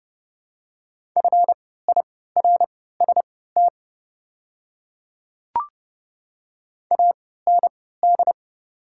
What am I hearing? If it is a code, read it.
Morse code, heard as a keyed tone, about 30 wpm, spaced as for 17 wpm, 700 Hz. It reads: FIRST AND
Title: 3x Character Spacing - Top 100 Words - 30wpm